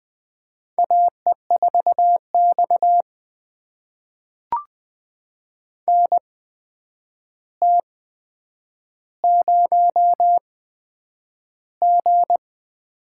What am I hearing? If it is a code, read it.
Morse code, heard as a keyed tone, about 20 wpm, spaced as for 9 wpm, 700 Hz. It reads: AE4X NT0G
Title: USA Call Signs - 4 Characters - 8x Character Spacing - 20wpm